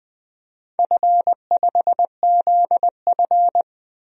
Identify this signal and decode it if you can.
Morse code, heard as a keyed tone, about 20 wpm, 700 Hz. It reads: F5ZF
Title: DXCC-100 Call Signs - 4 Characters - 20wpm